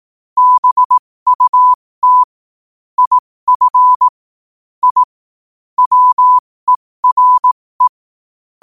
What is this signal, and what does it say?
Morse code, heard as a keyed tone, about 18 wpm, spaced as for 15 wpm, 1000 Hz. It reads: BUT IF I WERE
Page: New quotes every day in morse code at 15 Words per minute.